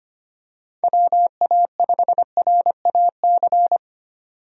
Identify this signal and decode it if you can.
Morse code, heard as a keyed tone, about 25 wpm, 700 Hz. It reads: WA5RAC
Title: USA Call Signs - 6 Characters - 25wpm